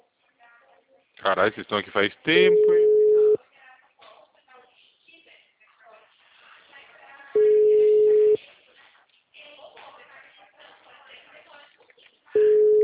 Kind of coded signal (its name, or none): telephone call progress tone